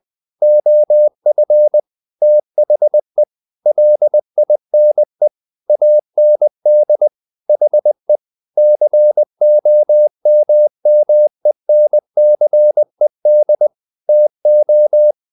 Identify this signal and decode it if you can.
Morse code, heard as a keyed tone, about 20 wpm, 600 Hz. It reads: OF THE LINE AND HE COMMENCED TO
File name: the old man and the sea part 2 20wpm